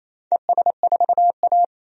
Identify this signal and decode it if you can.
Morse code, heard as a keyed tone, about 28 wpm, 700 Hz. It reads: ES4A
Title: DXCC-100 Call Signs - 4 Characters - 28wpm